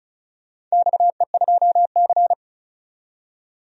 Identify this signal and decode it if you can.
Morse code, heard as a keyed tone, about 36 wpm, 700 Hz. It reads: XE2C
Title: DXCC-100 Call Signs - 4 Characters - 35wpm